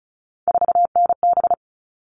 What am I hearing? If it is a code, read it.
Morse code, heard as a keyed tone, about 34 wpm, 700 Hz. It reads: 4NB
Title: All Cut Numbers - 3-Digit 35wpm